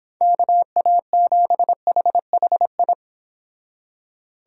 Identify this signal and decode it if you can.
Morse code, heard as a keyed tone, about 31 wpm, 700 Hz. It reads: KA7HHI